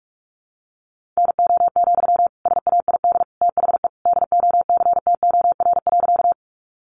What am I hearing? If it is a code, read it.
Morse code, heard as a keyed tone, about 44 wpm, 700 Hz. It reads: NO, SAID THE DOCTOR.
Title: Sentences from Top 1500 Words 45wpm